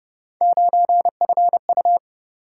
Morse code, heard as a keyed tone, about 31 wpm, 700 Hz. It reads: 9FU